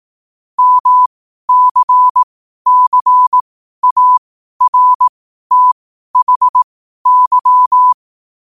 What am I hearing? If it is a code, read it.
Morse code, heard as a keyed tone, about 18 wpm, spaced as for 12 wpm, 1000 Hz. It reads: MCCARTHY
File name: qotdfiles / 12wpm